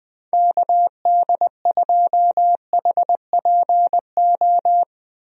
Morse code, heard as a keyed tone, about 20 wpm, 700 Hz. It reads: KD2HPO
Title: USA Call Signs - 6 Characters - Repeated 2x - 20wpm